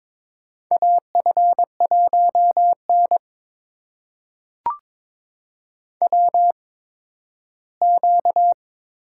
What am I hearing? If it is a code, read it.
Morse code, heard as a keyed tone, about 22 wpm, spaced as for 13 wpm, 700 Hz. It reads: AF1N WQ
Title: USA Call Signs - 4 Characters - 8x Character Spacing - 22wpm